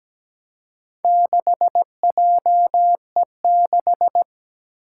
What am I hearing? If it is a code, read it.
Morse code, heard as a keyed tone, about 17 wpm, 700 Hz. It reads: 6JE6
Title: Four Challenging Letter-Number Combinations 17wpm